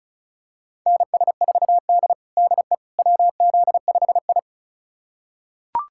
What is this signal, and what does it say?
Morse code, heard as a keyed tone, about 36 wpm, 700 Hz. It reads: NS4D DE WZ5I
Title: USA Callsign DE Callsign - 4 Characters - 35wpm